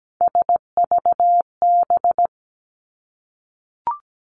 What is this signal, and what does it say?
Morse code, heard as a keyed tone, about 17 wpm, 700 Hz. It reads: SVB
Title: Three Challenging Letter-Number Combinations 17wpm